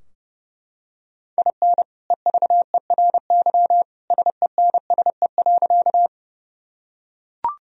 Morse code, heard as a keyed tone, about 32 wpm, 700 Hz. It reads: IN EVERY SENSE.
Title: Sentences from Top 1000 Words - Review Part 4 - 30wpm